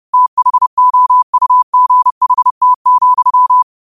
Morse code, heard as a keyed tone, about 30 wpm, 1000 Hz. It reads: THOUGHT,
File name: qotdfiles / 30wpm